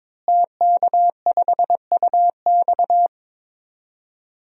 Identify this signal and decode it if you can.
Morse code, heard as a keyed tone, about 23 wpm, 700 Hz. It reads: TK5UX
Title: DXCC-100 Call Signs - 5 Characters - Repeated 3x - 22wpm